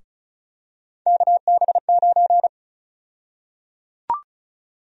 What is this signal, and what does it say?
Morse code, heard as a keyed tone, about 34 wpm, 700 Hz. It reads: KB9
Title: USA Call Sign Prefixes - Repeated 2x - 35wpm